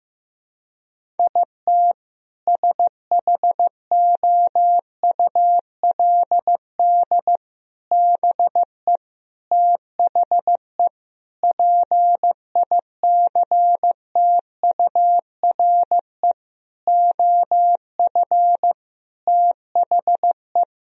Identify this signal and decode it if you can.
Morse code, heard as a keyed tone, about 15 wpm, 700 Hz. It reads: IT SHOULD BE THE PICTURE OF THE
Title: Sentences from Top 1000 Words - Review Part 1 - 15wpm